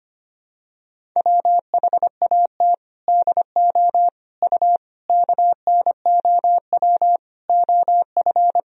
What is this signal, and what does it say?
Morse code, heard as a keyed tone, about 25 wpm, 700 Hz. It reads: WHAT DO U KNOW OF
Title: Sentences from Top 100 Words and Abbreviations 25wpm